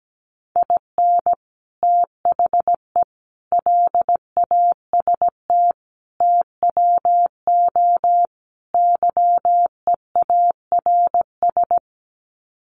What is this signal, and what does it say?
Morse code, heard as a keyed tone, about 17 wpm, 700 Hz. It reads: IN THE LAST TWO YEARS
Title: Sets of 5 Words 17wpm